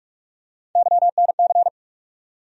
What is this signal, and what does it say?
Morse code, heard as a keyed tone, about 41 wpm, 700 Hz. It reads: YNC
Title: USA Call Sign Postfixes - 3 Letters 45wpm